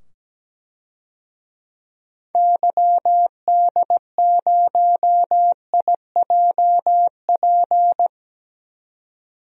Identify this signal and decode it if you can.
Morse code, heard as a keyed tone, about 17 wpm, 700 Hz. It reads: YD0IJP
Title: DXCC-100 Call Signs - 6 Characters - 17wpm